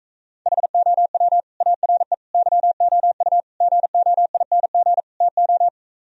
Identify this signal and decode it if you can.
Morse code, heard as a keyed tone, about 44 wpm, 700 Hz. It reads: HOW ARE YOU GOING TO